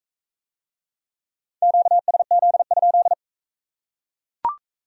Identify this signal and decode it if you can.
Morse code, heard as a keyed tone, about 41 wpm, 700 Hz. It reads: QSZ?